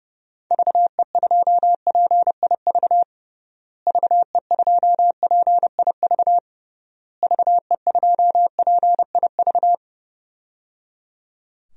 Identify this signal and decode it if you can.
Morse code, heard as a keyed tone, about 30 wpm, 700 Hz. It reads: VE2PIV VE2PIV VE2PIV
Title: DXCC-100 Call Signs - 6 Characters - Repeated 3x - 30wpm